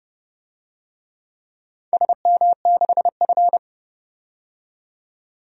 Morse code, heard as a keyed tone, about 30 wpm, 700 Hz. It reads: SM6F
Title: DXCC-100 Call Signs - 4 Characters - 30wpm